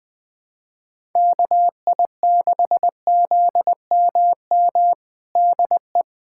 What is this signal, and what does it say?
Morse code, heard as a keyed tone, about 20 wpm, 700 Hz. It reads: KI6ZMM DE
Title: USA Callsign DE Callsign 20wpm